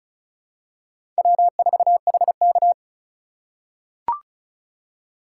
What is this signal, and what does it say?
Morse code, heard as a keyed tone, about 36 wpm, 700 Hz. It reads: W4HK
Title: USA Call Signs - 4 Characters - 4x Character Spacing - 35wpm